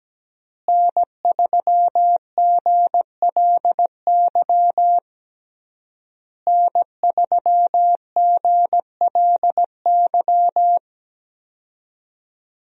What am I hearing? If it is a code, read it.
Morse code, heard as a keyed tone, about 17 wpm, 700 Hz. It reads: N3GLY N3GLY
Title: USA Call Signs - 5 Characters - Repeated 3x - 17wpm